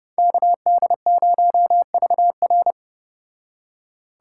Morse code, heard as a keyed tone, about 30 wpm, 700 Hz. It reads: KD0VR